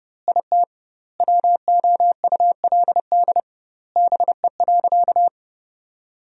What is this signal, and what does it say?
Morse code, heard as a keyed tone, about 30 wpm, 700 Hz. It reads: IT WOULD BE.
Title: 2x Word-Spacing - Sentences from Top 200 Words - 30wpm